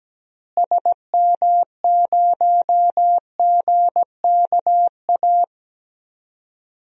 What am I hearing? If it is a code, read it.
Morse code, heard as a keyed tone, about 17 wpm, 700 Hz. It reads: SM0GKA
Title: DXCC-100 Call Signs - 6 Characters - Repeated 3x - 17wpm